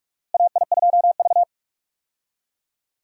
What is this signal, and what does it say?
Morse code, heard as a keyed tone, about 45 wpm, 700 Hz. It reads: AI2V
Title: USA Call Signs - 4 Characters - Repeated 3x - 45wpm